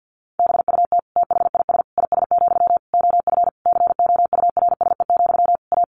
Morse code, heard as a keyed tone, about 50 wpm, 700 Hz. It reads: BUT THIS IS, OF COURSE, A
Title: Sentences from Top 1500 Words 50wpm